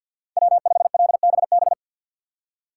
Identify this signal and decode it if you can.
Morse code, heard as a keyed tone, about 51 wpm, 700 Hz. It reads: W5LBB
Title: USA Call Signs - 5 Characters - 3x Character Spacing - 50wpm